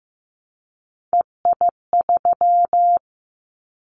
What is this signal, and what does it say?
Morse code, heard as a keyed tone, about 15 wpm, 700 Hz. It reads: EI3
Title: Callsign Prefixes for Top 50 DXCC - 15wpm